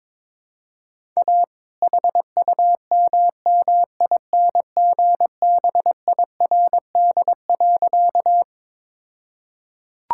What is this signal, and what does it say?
Morse code, heard as a keyed tone, about 22 wpm, 700 Hz. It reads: A HUMMINGBIRD.
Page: morse-code-ninja-practice-22wpm